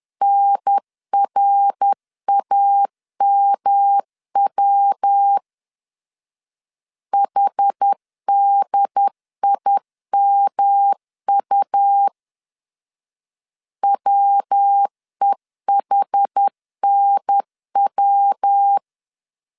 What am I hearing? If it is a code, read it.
Morse code, heard as a keyed tone, about 10 wpm, 800 Hz. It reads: NRAMW HDIMU WEHNW
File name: CW / DET / 1 3 1500C S B 10wpm 3335